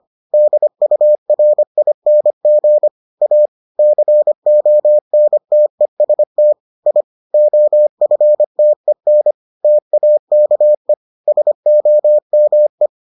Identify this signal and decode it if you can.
Morse code, heard as a keyed tone, about 25 wpm, 600 Hz. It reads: DURING A CONTEST I OFTEN TAKE SOME